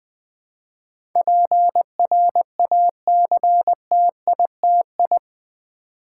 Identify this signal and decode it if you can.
Morse code, heard as a keyed tone, about 20 wpm, 700 Hz. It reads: PRACTITI
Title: Build A Word - 4 Letters At A Time - 12-Letter Words - 20wpm